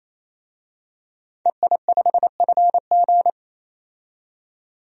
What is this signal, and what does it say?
Morse code, heard as a keyed tone, about 28 wpm, 700 Hz. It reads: EI5FG
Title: DXCC-100 Call Signs - 5 Characters - 28wpm